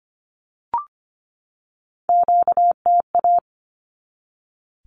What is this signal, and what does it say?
Morse code, heard as a keyed tone, about 25 wpm, 700 Hz. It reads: QTA
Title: Q-Codes 25wpm